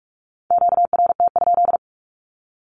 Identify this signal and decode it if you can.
Morse code, heard as a keyed tone, about 46 wpm, 700 Hz. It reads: QRT?
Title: Q-Codes with Spoken Meaning - 45wpm